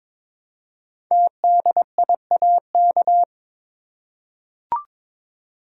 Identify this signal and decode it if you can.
Morse code, heard as a keyed tone, about 23 wpm, 700 Hz. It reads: TDIAK
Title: Code Groups - 5 Letters - 22wpm